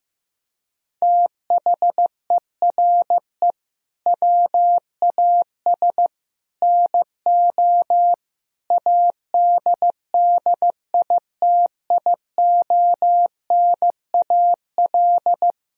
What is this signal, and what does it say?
Morse code, heard as a keyed tone, about 15 wpm, 700 Hz. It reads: THERE WAS NO ADDITIONAL
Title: Sentences from Top 1000 Words - Review Part 4 - 15wpm